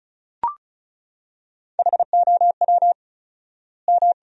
Morse code, heard as a keyed tone, about 36 wpm, 700 Hz. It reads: HOW M
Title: 4x Word-Spacing - Sentences from Top 300 Words - 35wpm